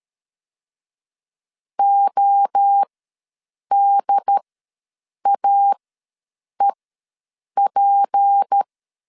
Morse code, heard as a keyed tone, about 13 wpm, spaced as for 7 wpm, 800 Hz. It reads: ODAEP
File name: LEZIONI / 7 GF9Z / 7 GF ws 12wpm 3140